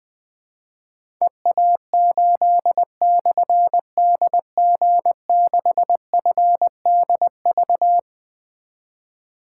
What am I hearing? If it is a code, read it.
Morse code, heard as a keyed tone, about 20 wpm, 700 Hz. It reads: EA8/DG6FDV
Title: Difficult Call Signs 20wpm